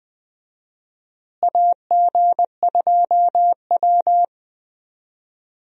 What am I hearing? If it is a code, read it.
Morse code, heard as a keyed tone, about 20 wpm, 700 Hz. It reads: AG2W
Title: USA Call Signs - 4 Characters - Repeated 3x - 20wpm